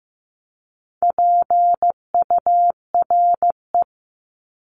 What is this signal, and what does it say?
Morse code, heard as a keyed tone, about 15 wpm, 700 Hz. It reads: PURE
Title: LY Suffix - Top 100 Words with Base Words 15wpm